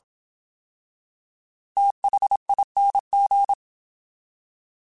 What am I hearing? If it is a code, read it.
Morse code, heard as a keyed tone, about 27 wpm, 800 Hz. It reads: THING
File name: TH / TH mp3s / TH Prefixes 3x2 Twice @25